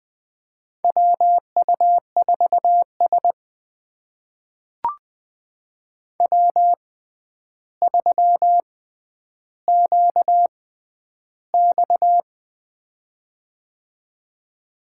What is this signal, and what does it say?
Morse code, heard as a keyed tone, about 20 wpm, spaced as for 10 wpm, 700 Hz. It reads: WU4S W3QX
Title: USA Call Signs - 6x Character Spacing - 20wpm